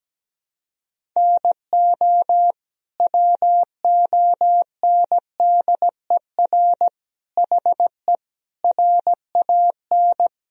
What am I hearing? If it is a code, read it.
Morse code, heard as a keyed tone, about 17 wpm, 700 Hz. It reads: NO WONDER HE RAN
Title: Sentences from Top 1000 Words - Review Part 3 - 17wpm